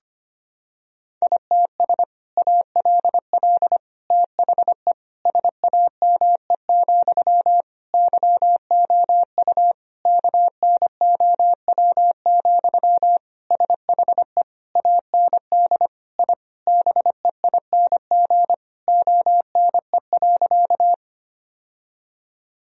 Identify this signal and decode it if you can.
Morse code, heard as a keyed tone, about 25 wpm, 700 Hz. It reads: ITS ALL THE SAME, YOU KNOW, SHE AND I BEING ONE.
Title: Sentences from Top 1000 Words - Review Part 3 - 25wpm